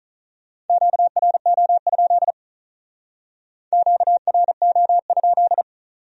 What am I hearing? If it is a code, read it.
Morse code, heard as a keyed tone, about 45 wpm, 700 Hz. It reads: QRO? QRO?